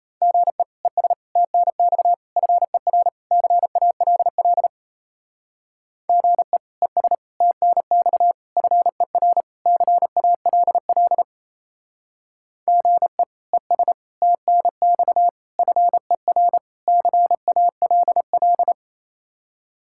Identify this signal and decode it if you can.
Morse code, heard as a keyed tone, about 34 wpm, 700 Hz. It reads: GE ES TNX FER CALL GE ES TNX FER CALL GE ES TNX FER CALL
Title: Speed-Racing Common QSO Phrases Encore - 28wpm